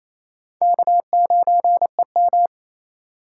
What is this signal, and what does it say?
Morse code, heard as a keyed tone, about 39 wpm, spaced as for 34 wpm, 700 Hz. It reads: K9EM
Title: Speed Racing - USA 1x2 Call Signs 28wpm